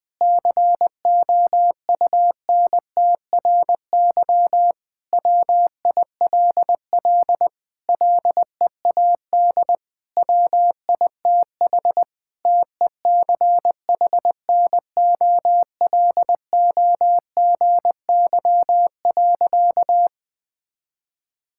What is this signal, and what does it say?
Morse code, heard as a keyed tone, about 20 wpm, 700 Hz. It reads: COUNTRY WILL LEAD WITH TECHNOLOGY.
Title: Sentences from Top 1000 Words - Review Part 5 - 20wpm